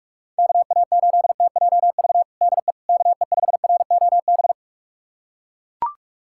Morse code, heard as a keyed tone, about 41 wpm, 700 Hz. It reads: KA8TJV DE KE5ROB
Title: USA Callsign DE Callsign 45wpm